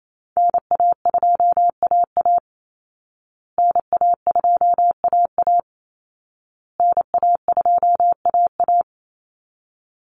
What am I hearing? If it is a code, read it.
Morse code, heard as a keyed tone, about 28 wpm, 700 Hz. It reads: NA2AA NA2AA NA2AA
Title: Call Signs Repeated 2x - 28wpm